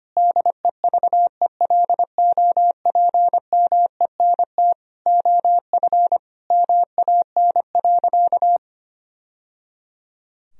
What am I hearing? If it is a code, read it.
Morse code, heard as a keyed tone, about 25 wpm, 700 Hz. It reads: DEVELOPMENT OF MAN.
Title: Sentences from Top 1000 Words - Review Part 4 - 25wpm